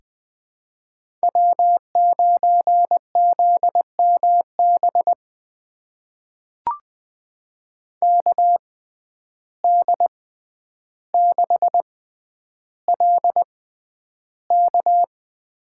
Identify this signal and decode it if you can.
Morse code, heard as a keyed tone, about 20 wpm, spaced as for 10 wpm, 700 Hz. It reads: W9ZMB KD6LK
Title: USA Call Signs - 5 Characters - 6x Character Spacing - 20wpm